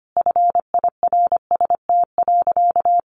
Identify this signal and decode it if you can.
Morse code, heard as a keyed tone, about 25 wpm, 700 Hz. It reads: FIRST.
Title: Sentences from Top 300 Words 25wpm